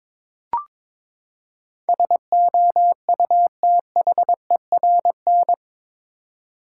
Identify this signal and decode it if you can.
Morse code, heard as a keyed tone, about 22 wpm, 700 Hz. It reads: SOUTHERN
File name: practice / 22wpm / Eight Letter Words - 22wpm